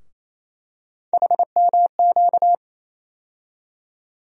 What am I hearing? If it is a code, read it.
Morse code, heard as a keyed tone, about 27 wpm, 700 Hz. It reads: HMQ
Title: USA Call Sign Postfixes - 3 Letters 28wpm